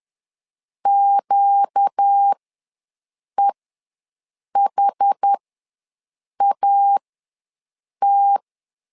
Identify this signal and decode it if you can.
Morse code, heard as a keyed tone, about 10 wpm, spaced as for 6 wpm, 800 Hz. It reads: QEHAT
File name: LEZIONI / 4 Q ws 10wpm 2306